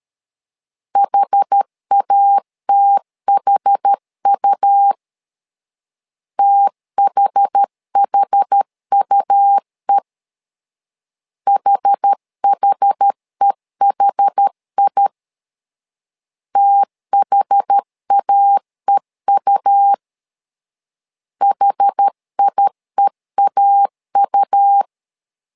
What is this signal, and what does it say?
Morse code, heard as a keyed tone, about 13 wpm, 800 Hz. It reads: HATHU THHUE HHEHI THAEU HIEAU
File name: LEZIONI / 2 UHDN / 2 UH 12wpm 1439